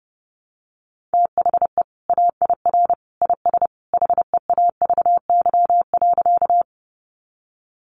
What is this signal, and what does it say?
Morse code, heard as a keyed tone, about 30 wpm, 700 Hz. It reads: THE AIR IS HEAVY.
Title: Sentences from Top 1100 Words 30wpm